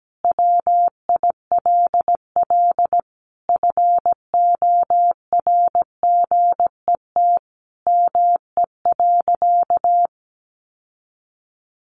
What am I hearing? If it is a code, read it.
Morse code, heard as a keyed tone, about 17 wpm, 700 Hz. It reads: WILL FORGET ME.
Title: Sentences from Top 1200 Words 17wpm